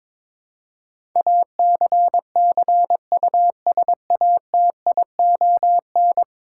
Morse code, heard as a keyed tone, about 21 wpm, 700 Hz. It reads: ACCUSATION
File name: practice / 22wpm / 10 Letter Words - First 3 Letters Given - 22wpm